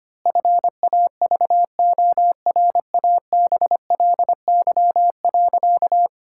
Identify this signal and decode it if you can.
Morse code, heard as a keyed tone, about 25 wpm, 700 Hz. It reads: FAVORABLY.
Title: Pangram Challenge v2 - 25wpm